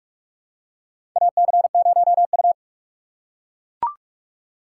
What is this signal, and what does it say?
Morse code, heard as a keyed tone, about 44 wpm, 690 Hz. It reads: AC0U
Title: USA Call Signs - 4 Characters - 2x Character Spacing - 45wpm